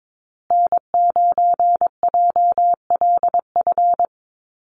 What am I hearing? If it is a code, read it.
Morse code, heard as a keyed tone, about 22 wpm, 700 Hz. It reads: N9JLF
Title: USA Call Signs Repeated 2x 22wpm